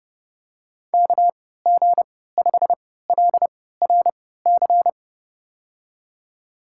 Morse code, heard as a keyed tone, about 30 wpm, spaced as for 17 wpm, 700 Hz. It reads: KG5LRC
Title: USA Call Signs - 3x Character Spacing - 30wpm